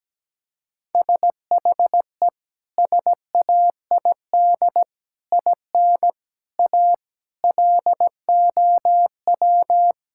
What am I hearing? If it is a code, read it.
Morse code, heard as a keyed tone, about 17 wpm, 700 Hz. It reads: SHE SAID IN A LOW
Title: Sentences from Top 1600 Words 17wpm